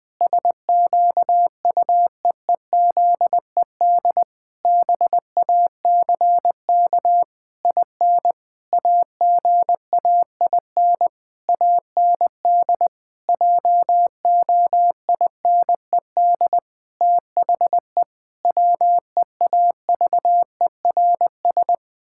Morse code, heard as a keyed tone, about 20 wpm, 700 Hz. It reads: SQUEEZED BACK IN AGAIN AND JOINED THE WEAVERS
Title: Pangram Challenge v2 - 20wpm